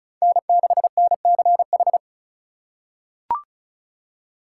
Morse code, heard as a keyed tone, about 36 wpm, 700 Hz. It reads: N6NCH
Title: USA Call Signs - 5 Characters - 2x Character Spacing - 35wpm